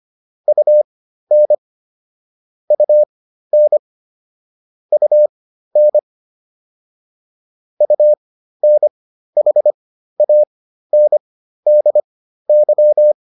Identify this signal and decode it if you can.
Morse code, heard as a keyed tone, about 25 wpm, spaced as for 13 wpm, 600 Hz. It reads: UN UN UN UNHANDY